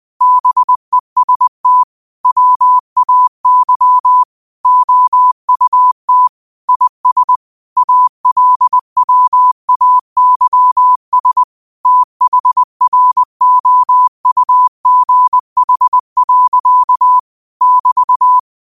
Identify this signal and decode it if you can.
Morse code, heard as a keyed tone, about 20 wpm, 1000 Hz. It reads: BEST WAY OUT IS ALWAYS THROUGH. <BT>
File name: qotdfiles / 20wpm